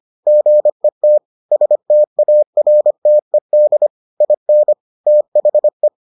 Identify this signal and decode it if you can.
Morse code, heard as a keyed tone, about 25 wpm, 600 Hz. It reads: GET STARTED IN THE